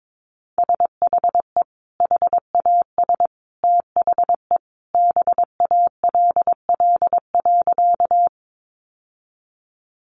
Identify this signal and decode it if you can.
Morse code, heard as a keyed tone, about 22 wpm, 700 Hz. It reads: SHE HAS THE BALL.
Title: Sentences from Top 900 Words 22wpm